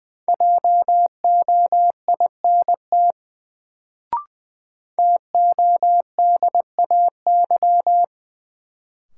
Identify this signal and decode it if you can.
Morse code, heard as a keyed tone, about 20 wpm, 700 Hz. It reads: JOINT TODAY
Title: Sets of 5 Words Encore - New Word Drill - 20wpm